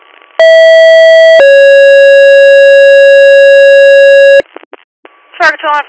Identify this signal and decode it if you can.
Two-tone page, heard as a one-second tone, then a three-second tone, about 650 then 550 Hz, with speech. It is heard from 0.4 to 4.4 s.